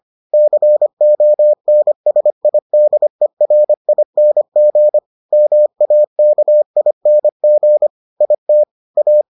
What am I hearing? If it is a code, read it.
Morse code, heard as a keyed tone, about 25 wpm, 600 Hz. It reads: CONSIDERING MAKING IT A